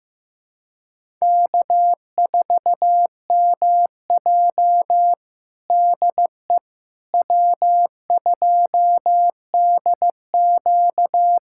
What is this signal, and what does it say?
Morse code, heard as a keyed tone, about 15 wpm, 700 Hz. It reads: K4MJ DE W2DQ
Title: USA Callsign DE Callsign - 4 Characters - 15wpm